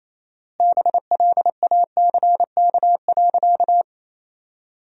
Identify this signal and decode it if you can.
Morse code, heard as a keyed tone, about 28 wpm, 700 Hz. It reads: BLACK.
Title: 4x Word-Spacing - Sentences from Top 400 Words - 28wpm